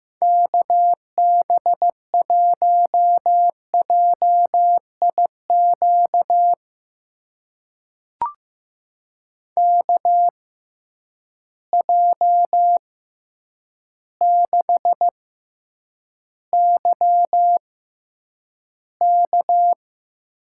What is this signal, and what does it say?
Morse code, heard as a keyed tone, about 15 wpm, spaced as for 8 wpm, 700 Hz. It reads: KB1JIQ KJ6YK
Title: USA Call Signs - 6 Characters - 6x Character Spacing - 15wpm